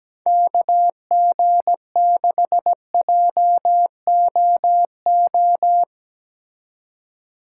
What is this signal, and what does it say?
Morse code, heard as a keyed tone, about 17 wpm, 700 Hz. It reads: KG6JOO
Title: Speed Racing - USA 2x3 Call Signs 17wpm